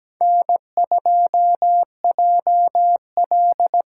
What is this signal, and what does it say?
Morse code, heard as a keyed tone, about 20 wpm, 700 Hz. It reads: N2JL
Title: Speed Racing - USA 1x2 Call Signs 17wpm